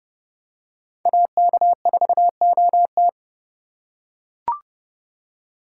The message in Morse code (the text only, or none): AK4OT